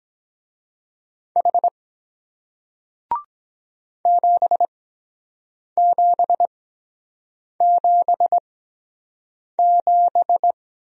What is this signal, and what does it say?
Morse code, heard as a keyed tone, about 23 wpm, spaced as for 7 wpm, 700 Hz. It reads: H 7777